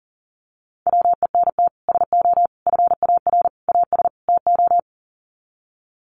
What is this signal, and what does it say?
Morse code, heard as a keyed tone, about 40 wpm, 700 Hz. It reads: WENT SO FAR AS TO
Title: Sets of 5 Words Encore 40wpm